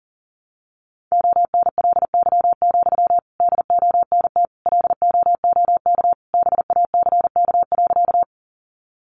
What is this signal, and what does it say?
Morse code, heard as a keyed tone, about 40 wpm, 700 Hz. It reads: ONLY, DONT LOOK BACK.
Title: Sentences from Top 300 Words 40wpm